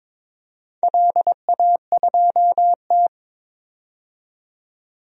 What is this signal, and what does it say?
Morse code, heard as a keyed tone, about 21 wpm, 700 Hz. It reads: LA2T
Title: DXCC-100 Call Signs - 4 Characters - 22wpm